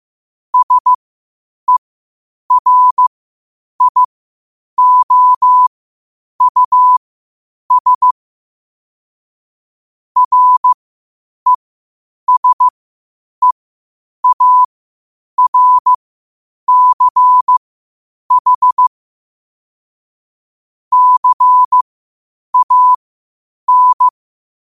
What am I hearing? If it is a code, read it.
Morse code, heard as a keyed tone, about 15 wpm, spaced as for 8 wpm, 1000 Hz. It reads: SERIOUS RESEARCH CAN
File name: qotdfiles / 8wpm